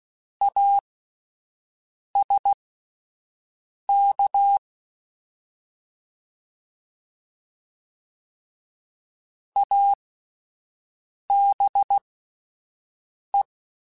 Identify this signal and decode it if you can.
Morse code, heard as a keyed tone, about 16 wpm, spaced as for 5 wpm, 800 Hz. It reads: ASK ABE
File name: Alphabetical / Alphabetical mp3s / As @5